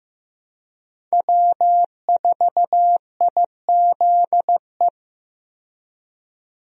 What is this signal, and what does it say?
Morse code, heard as a keyed tone, about 15 wpm, 700 Hz. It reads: W4IZE
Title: USA Call Signs - 5 Characters - 15wpm